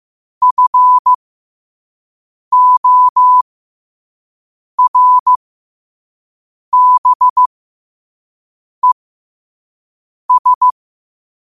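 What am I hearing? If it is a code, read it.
Morse code, heard as a keyed tone, about 15 wpm, spaced as for 5 wpm, 1000 Hz. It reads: FORBES
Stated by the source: Morse code sent at 5 wpm for learning.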